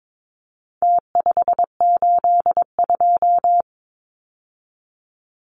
Morse code, heard as a keyed tone, about 22 wpm, 700 Hz. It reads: T582
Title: Numbers - Contest Style - 4-Digit 22wpm